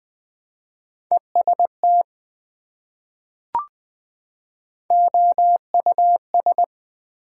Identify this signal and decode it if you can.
Morse code, heard as a keyed tone, about 21 wpm, 700 Hz. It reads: EST OUS
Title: Prefixes and Suffixes 20wpm